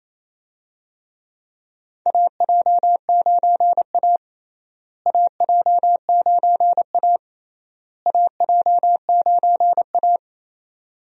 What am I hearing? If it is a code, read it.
Morse code, heard as a keyed tone, about 28 wpm, 700 Hz. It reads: AJ9A AJ9A AJ9A
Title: USA Call Signs - 4 Characters - Repeated 3x - 28wpm